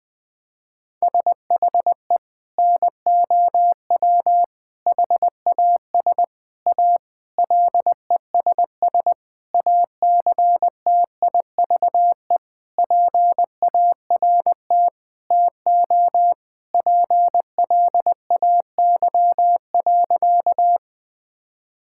Morse code, heard as a keyed tone, about 21 wpm, 700 Hz. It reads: SHE NOW HAS A LESS ACTIVE PART TO PLAY.
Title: Sentences from Top 1000 Words - Review Part 6 - 20wpm